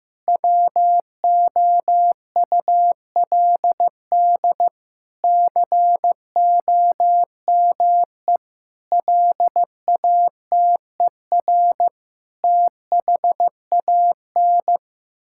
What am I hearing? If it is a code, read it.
Morse code, heard as a keyed tone, about 15 wpm, 700 Hz. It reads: WOULD COME LATER THAN
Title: Sentences from Top 1600 Words 15wpm